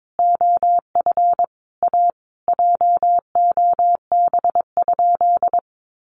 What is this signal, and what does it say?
Morse code, heard as a keyed tone, about 22 wpm, 700 Hz. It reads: OF A JOB?
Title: Sentences from Top 1300 Words 22wpm